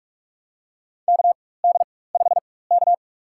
Morse code, heard as a keyed tone, about 46 wpm, spaced as for 21 wpm, 700 Hz. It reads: KD5K
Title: USA Call Signs - 4 Characters - 4x Character Spacing - 45wpm